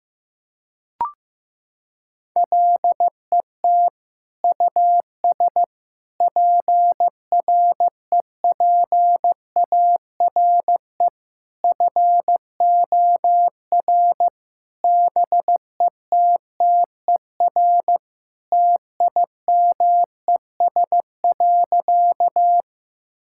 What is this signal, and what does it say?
Morse code, heard as a keyed tone, about 15 wpm, 700 Hz. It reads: LET US PREPARE FOR BETTER TIMES.
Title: Sentences from Top 1000 Words - Review Part 6 - 15wpm